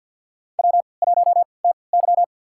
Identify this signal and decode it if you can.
Morse code, heard as a keyed tone, about 53 wpm, spaced as for 29 wpm, 700 Hz. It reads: W1TY
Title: USA Call Signs - 3x Character Spacing - 50wpm